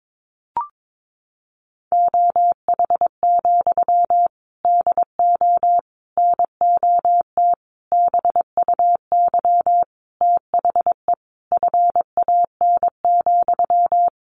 Morse code, heard as a keyed tone, about 22 wpm, 700 Hz. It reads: OH, DO NOT BUY THE FAN,
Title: Sentences from Top 1300 Words 22wpm